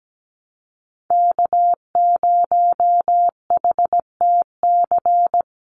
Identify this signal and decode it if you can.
Morse code, heard as a keyed tone, about 17 wpm, 700 Hz. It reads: K0HTC
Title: USA Call Signs 17wpm